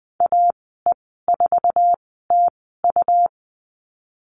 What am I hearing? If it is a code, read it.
Morse code, heard as a keyed tone, about 20 wpm, spaced as for 15 wpm, 700 Hz. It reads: AE4TU
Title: 122 Lesson 20g - Number 3 - Call Signs 20wpm